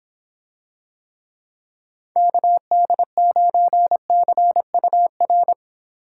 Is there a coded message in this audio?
Yes: Morse code from 2.2 to 5.5 s, heard as a keyed tone, about 25 wpm, 700 Hz. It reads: KD9CUR